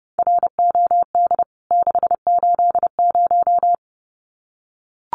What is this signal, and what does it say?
Morse code, heard as a keyed tone, about 30 wpm, 700 Hz. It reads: ROD 680
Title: CWT Exchange 30wpm